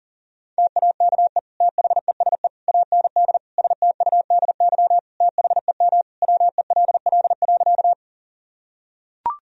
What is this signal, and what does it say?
Morse code, heard as a keyed tone, about 40 wpm, 700 Hz. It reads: TAKE THESE AND STUDY THEM WELL.
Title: Sentences from Top 1000 Words - Review Part 1 - 40wpm